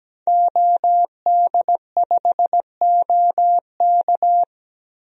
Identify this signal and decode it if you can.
Morse code, heard as a keyed tone, about 17 wpm, 700 Hz. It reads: OD5OK
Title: DXCC-100 Call Signs - 5 Characters - 17wpm